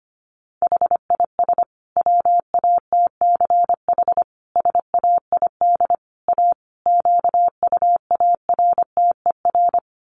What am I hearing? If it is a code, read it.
Morse code, heard as a keyed tone, about 25 wpm, 700 Hz. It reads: HIS WATCH SAID A QUARTER
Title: Sentences from Top 1300 Words 25wpm